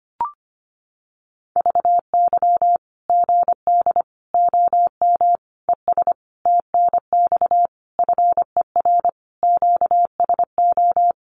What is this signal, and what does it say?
Morse code, heard as a keyed tone, about 25 wpm, 700 Hz. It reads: VY GD OM ES TNX FER QSO
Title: Common QSO Phrases 25wpm